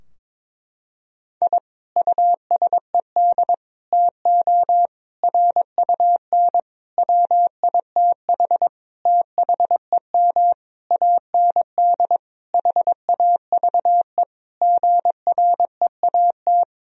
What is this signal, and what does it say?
Morse code, heard as a keyed tone, about 22 wpm, 700 Hz. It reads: I USED TO RUN WITH THEM AND HAVE GREAT
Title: Sentences from Top 1000 Words - Review Part 4 - 22wpm